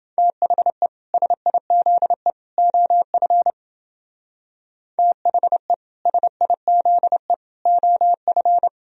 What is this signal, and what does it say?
Morse code, heard as a keyed tone, about 28 wpm, 700 Hz. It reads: THE SIZE OF THE SIZE OF